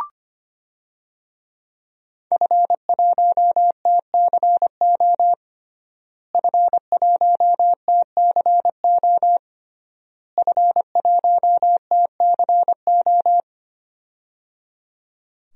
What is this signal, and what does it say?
Morse code, heard as a keyed tone, about 25 wpm, 700 Hz. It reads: F1TCO F1TCO F1TCO
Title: DXCC-100 Call Signs - 5 Characters - Repeated 3x - 25wpm